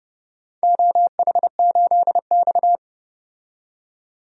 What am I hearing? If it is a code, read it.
Morse code, heard as a keyed tone, about 30 wpm, 700 Hz. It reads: OH8X
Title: DXCC-100 Call Signs - 4 Characters - 30wpm